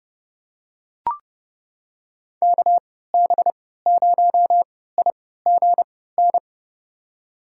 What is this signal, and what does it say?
Morse code, heard as a keyed tone, about 30 wpm, spaced as for 17 wpm, 700 Hz. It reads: KB0IGN
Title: USA Call Signs - 6 Characters - 3x Character Spacing - 30wpm